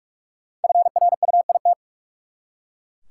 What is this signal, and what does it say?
Morse code, heard as a keyed tone, about 43 wpm, 700 Hz. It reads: FRUIT
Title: Five Letter Words - 45wpm